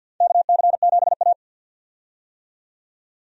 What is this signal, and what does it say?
Morse code, heard as a keyed tone, about 50 wpm, 700 Hz. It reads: KC7A
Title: USA Call Signs - 4 Characters - 50wpm